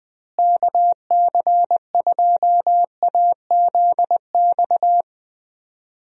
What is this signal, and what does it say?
Morse code, heard as a keyed tone, about 20 wpm, 700 Hz. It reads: KC2AZX
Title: USA Call Signs - 6 Characters - Repeated 3x - 20wpm